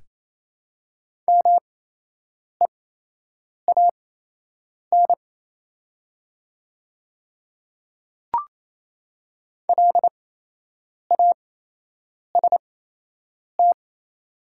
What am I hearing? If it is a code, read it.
Morse code, heard as a keyed tone, about 27 wpm, spaced as for 8 wpm, 700 Hz. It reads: MEAN LAST
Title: 8x Character Spacing - Top 300 Words - 28wpm